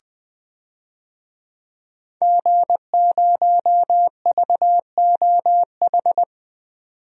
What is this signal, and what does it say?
Morse code, heard as a keyed tone, about 20 wpm, 700 Hz. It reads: G0VOH
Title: DXCC-100 Call Signs - 5 Characters - 20wpm